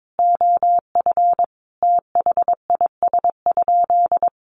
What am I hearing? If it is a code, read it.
Morse code, heard as a keyed tone, about 22 wpm, 700 Hz. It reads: OF THIS?
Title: Sentences from Top 900 Words 22wpm